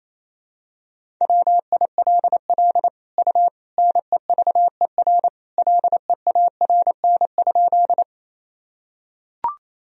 Morse code, heard as a keyed tone, about 27 wpm, 700 Hz. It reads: WILL U NEVER LEARN?
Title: Sentences from Top 300 Words and Abbreviations 28wpm